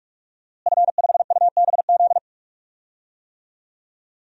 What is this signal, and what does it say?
Morse code, heard as a keyed tone, about 46 wpm, 700 Hz. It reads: F5UBZ
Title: DXCC-100 Call Signs - 5 Characters - Repeated 3x - 45wpm